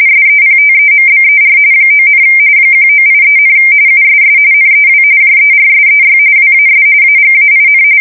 RTTY, about 45.45 baud. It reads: SEPTEMBER 9, 1983, USS NEW JERSEY WAS AGAIN
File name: fsk170 / battleshipnj